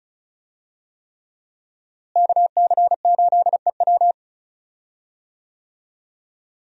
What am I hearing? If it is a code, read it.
Morse code, heard as a keyed tone, about 36 wpm, 700 Hz. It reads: KC8EW
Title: USA Call Signs - 5 Characters - Repeated 3x - 35wpm